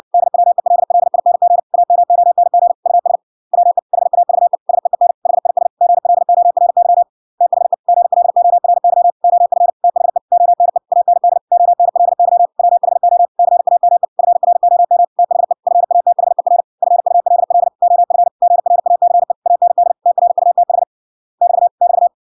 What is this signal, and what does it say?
Morse code, heard as a keyed tone, about 61 wpm, 700 Hz. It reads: DWELLING AMONG US WE HAVE SEEN HIS GLORY THE GLORY OF THE ONE AND ONLY WHO CAME FROM THE FATHER FULL OF GRACE AND TRUTH <BT> <BT>